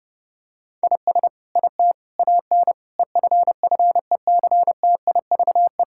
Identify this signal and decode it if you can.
Morse code, heard as a keyed tone, about 30 wpm, 700 Hz. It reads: IS IT AN EFFECTIVE
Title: Sentences from Top 1000 Words - Review Part 6 - 30wpm